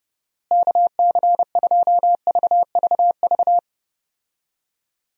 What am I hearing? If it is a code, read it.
Morse code, heard as a keyed tone, about 30 wpm, 700 Hz. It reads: KC2VVV
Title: USA Call Signs - 6 Characters - Repeated 2x - 30wpm